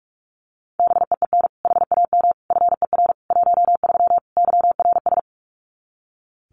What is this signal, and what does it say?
Morse code, heard as a keyed tone, about 44 wpm, 700 Hz. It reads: BEEN HAM FER 13 YRS
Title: Common QSO Phrases 45wpm